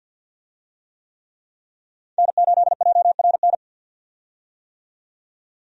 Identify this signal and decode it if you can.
Morse code, heard as a keyed tone, about 53 wpm, 700 Hz. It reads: N8JRN
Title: USA Call Signs - 5 Characters - 50wpm